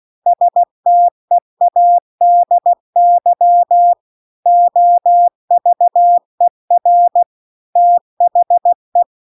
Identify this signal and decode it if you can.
Morse code, heard as a keyed tone, about 16 wpm, 700 Hz. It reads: STEADY OVER THE
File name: CW-The-Big-Twoheaded-River-2-16WPM